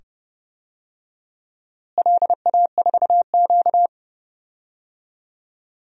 Morse code, heard as a keyed tone, about 30 wpm, 700 Hz. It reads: LA4Q
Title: DXCC-100 Call Signs - 4 Characters - Repeated 2x - 30wpm